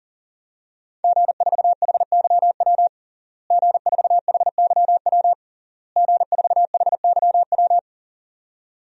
Morse code, heard as a keyed tone, about 40 wpm, 700 Hz. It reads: G4HYW G4HYW G4HYW
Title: DXCC-100 Call Signs - 5 Characters - Repeated 3x - 40wpm